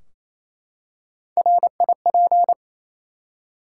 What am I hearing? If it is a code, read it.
Morse code, heard as a keyed tone, about 28 wpm, 700 Hz. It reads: RIP
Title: Three Letter Words - 28wpm